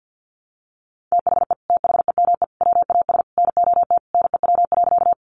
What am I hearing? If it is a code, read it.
Morse code, heard as a keyed tone, about 50 wpm, 700 Hz. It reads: THE THEME WAS NOT NEW.
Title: Sentences from Top 1100 Words 50wpm